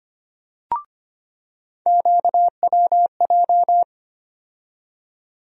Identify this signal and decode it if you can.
Morse code, heard as a keyed tone, about 25 wpm, 700 Hz. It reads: QWJ
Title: USA Call Sign Postfixes - 3 Letters 25wpm